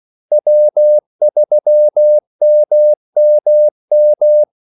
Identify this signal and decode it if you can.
Morse code, heard as a keyed tone, about 16 wpm, 600 Hz. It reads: W3MMM
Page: Kaldesignaler 16 wpm